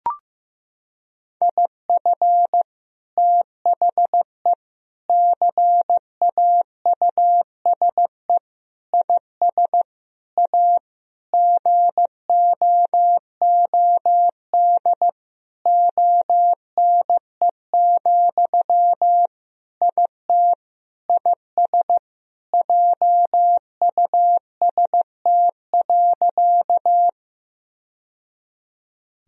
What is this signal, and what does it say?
Morse code, heard as a keyed tone, about 15 wpm, 700 Hz. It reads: IF THE CAUSE IS A GOOD ONE, IT IS JUST.
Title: Sentences from Top 1000 Words - Review Part 1 - 15wpm